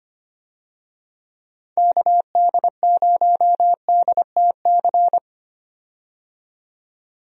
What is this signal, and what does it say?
Morse code, heard as a keyed tone, about 25 wpm, 700 Hz. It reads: KD0DTC
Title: USA Call Signs - 6 Characters - 25wpm